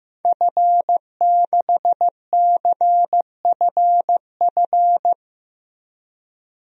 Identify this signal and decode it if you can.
Morse code, heard as a keyed tone, about 15 wpm, 700 Hz. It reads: F6CFF
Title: DXCC-100 Call Signs - 5 Characters - Repeated 3x - 15wpm